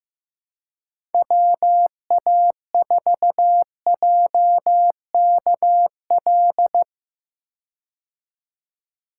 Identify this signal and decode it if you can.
Morse code, heard as a keyed tone, about 15 wpm, 700 Hz. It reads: WA4JKL
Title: K1USN SST QSO Practice - 15wpm